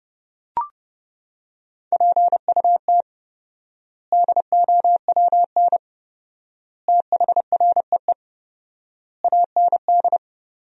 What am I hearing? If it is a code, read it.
Morse code, heard as a keyed tone, about 30 wpm, 700 Hz. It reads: PUT DOWN THREE AND
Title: 4x Word-Spacing - Sentences from Top 400 Words - 30wpm